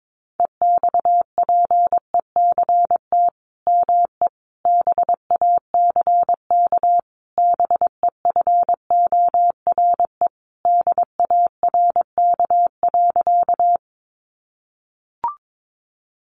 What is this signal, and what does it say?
Morse code, heard as a keyed tone, about 22 wpm, 700 Hz. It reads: EXPECT ME BACK BEFORE DARK.
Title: Sentences from Top 1100 Words 22wpm